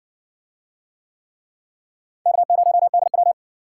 Morse code, heard as a keyed tone, about 60 wpm, 700 Hz. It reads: K0DW